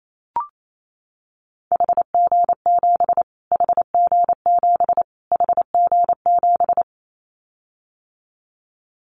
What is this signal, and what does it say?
Morse code, heard as a keyed tone, about 28 wpm, 700 Hz. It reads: HG7 HG7 HG7
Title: Callsign Prefixes for Top 25 DXCC - Repeated 3x - 28wpm